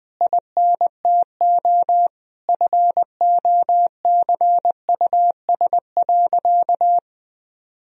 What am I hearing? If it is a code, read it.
Morse code, heard as a keyed tone, about 20 wpm, 700 Hz. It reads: INTO FOCUS.
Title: Sentences from Top 1000 Words - Review Part 4 - 20wpm